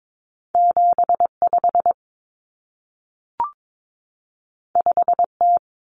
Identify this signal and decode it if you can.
Morse code, heard as a keyed tone, about 22 wpm, 700 Hz. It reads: 75 5T
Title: Common Cut Numbers - 2-Digit 22wpm